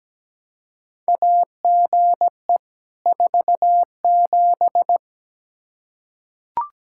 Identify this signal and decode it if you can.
Morse code, heard as a keyed tone, about 17 wpm, 700 Hz. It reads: AGE 47
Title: Practice QSO - Exchange 3 - More Info - 17wpm